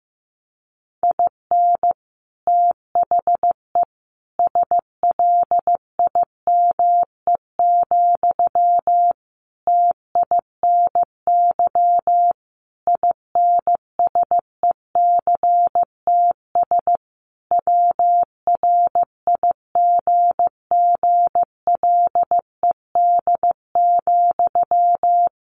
Morse code, heard as a keyed tone, about 14 wpm, 700 Hz. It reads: IN THE SLIME, TINY INSECTS WRIGGLED,
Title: A Sound Of Thunder 15wpm